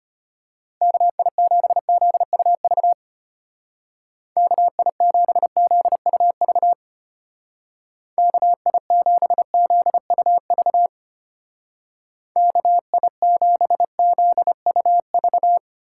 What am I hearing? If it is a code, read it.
Morse code, heard as a keyed tone, about 32 wpm, 700 Hz. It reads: KI7ZUV KI7ZUV KI7ZUV KI7ZUV